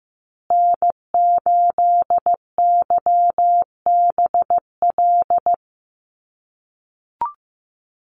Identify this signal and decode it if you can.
Morse code, heard as a keyed tone, about 15 wpm, 700 Hz. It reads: N8YBL
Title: USA Call Signs Repeated 3x 15wpm